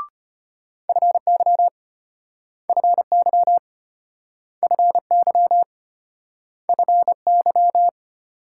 Morse code, heard as a keyed tone, about 34 wpm, 700 Hz. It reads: FY FY FY FY